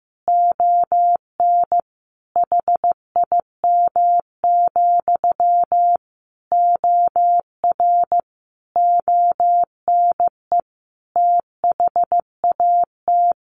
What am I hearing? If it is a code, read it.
Morse code, heard as a keyed tone, about 15 wpm, 700 Hz. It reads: ON HIM, OR ONE THAT
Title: A Sound Of Thunder 15wpm